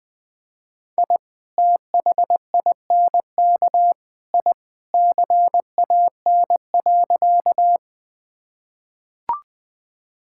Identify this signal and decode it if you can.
Morse code, heard as a keyed tone, about 20 wpm, 700 Hz. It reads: I THINK I CAN.
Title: Sentences with 3-Word Phrases - 20wpm